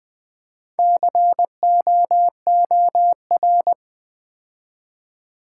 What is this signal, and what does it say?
Morse code, heard as a keyed tone, about 20 wpm, 700 Hz. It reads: COOR
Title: Build A Word - 4 Letters At A Time - 10-Letter Words - 20wpm